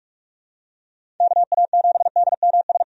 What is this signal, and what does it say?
Morse code, heard as a keyed tone, about 46 wpm, 700 Hz. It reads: KA7DMS
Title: USA Call Signs - 6 Characters - 45wpm